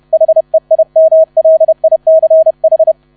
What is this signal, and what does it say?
Morse code, heard as a keyed tone, about 31 wpm, 640 Hz. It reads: HEIMLICH
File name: 30wpm